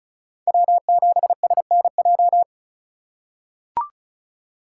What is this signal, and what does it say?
Morse code, heard as a keyed tone, about 36 wpm, 700 Hz. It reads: W7SNJ
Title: USA Call Signs - 5 Characters - 4x Character Spacing - 35wpm